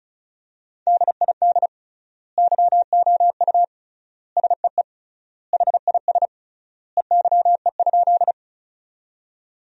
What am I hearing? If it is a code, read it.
Morse code, heard as a keyed tone, about 36 wpm, 700 Hz. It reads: DID YOU SEE HIS EYE?
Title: 3x Word-Spacing - Sentences from Top 300 Words - 35wpm